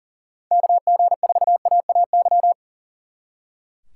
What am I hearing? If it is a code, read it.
Morse code, heard as a keyed tone, about 50 wpm, 700 Hz. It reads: KG4AAY
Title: Speed Racing - USA 2x3 Call Signs 40wpm